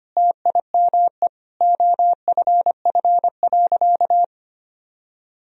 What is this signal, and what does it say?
Morse code, heard as a keyed tone, about 25 wpm, 700 Hz. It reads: TIME OFF.
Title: Sentences from Top 1000 Words - Review Part 3 - 25wpm